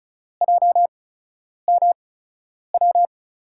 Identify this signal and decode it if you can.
Morse code, heard as a keyed tone, about 36 wpm, 700 Hz. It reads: J M W
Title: USA Call Signs - 6 Characters - 8x Character Spacing - 35wpm